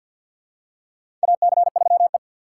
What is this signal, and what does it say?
Morse code, heard as a keyed tone, about 50 wpm, 700 Hz. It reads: AC3E
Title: USA Call Signs - 4x Character Spacing - 50wpm